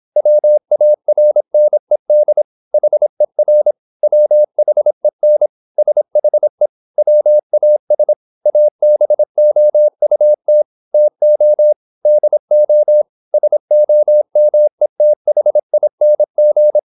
Morse code, heard as a keyed tone, about 26 wpm, 600 Hz. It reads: WARNED HER WHEN SHE WAS ABOUT TO DO SOMETHING
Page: Never 26wpm | CW med Gnister